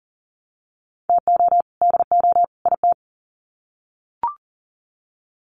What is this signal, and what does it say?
Morse code, heard as a keyed tone, about 40 wpm, 700 Hz. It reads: TO DO IT
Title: Sets of 3 Words 40wpm